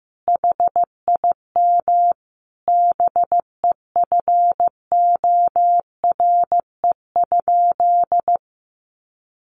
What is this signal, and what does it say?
Morse code, heard as a keyed tone, about 15 wpm, 700 Hz. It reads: HIM BEFORE?
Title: Sentences from Top 900 Words 15wpm